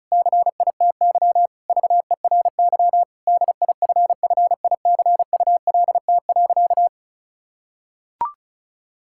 Morse code, heard as a keyed tone, about 36 wpm, 700 Hz. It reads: CITY VERY DIFFICULT.
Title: Sentences from Top 1000 Words - Review Part 6 - 35wpm